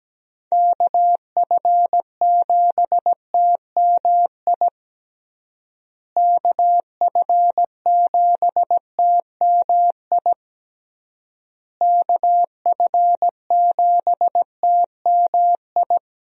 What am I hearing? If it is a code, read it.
Morse code, heard as a keyed tone, about 17 wpm, 700 Hz. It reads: KF7TMI KF7TMI KF7TMI
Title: USA Call Signs - 6 Characters - Repeated 3x - 17wpm